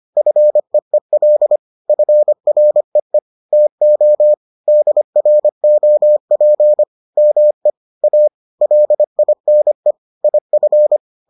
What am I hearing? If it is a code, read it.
Morse code, heard as a keyed tone, about 25 wpm, 600 Hz. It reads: FEEL FREE TO DROP ME A LINE IF